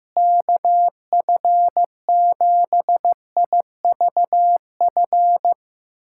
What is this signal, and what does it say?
Morse code, heard as a keyed tone, about 15 wpm, 700 Hz. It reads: KF7IVF
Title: USA Callsign DE Callsign 15wpm